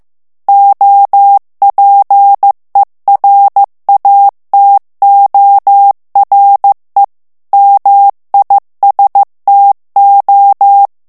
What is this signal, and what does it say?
Morse code, heard as a keyed tone, about 15 wpm, 800 Hz. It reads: OPERATORE MISTO
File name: morse mp3 / ITALIEN texte clair 2 vitesse 14 mots minute